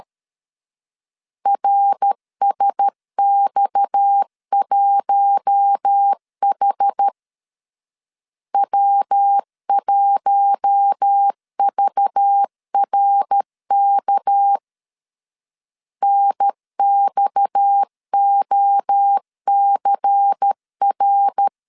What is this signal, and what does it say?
Morse code, heard as a keyed tone, about 13 wpm, 800 Hz. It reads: RSX1H W1VRK NXOCR
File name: LEZIONI / 6 PKX0 / 6 PKX 12wpm 2024